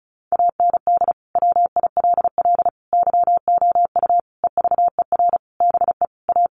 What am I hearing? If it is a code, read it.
Morse code, heard as a keyed tone, about 34 wpm, 700 Hz. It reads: AND WILL YOU EVER BE A
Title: Sentences from Top 1500 Words 35wpm